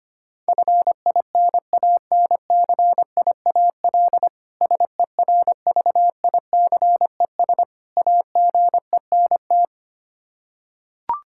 Morse code, heard as a keyed tone, about 25 wpm, 700 Hz. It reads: FINANCIAL SERVICES AGENT
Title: K1USN SST QSO Practice 25wpm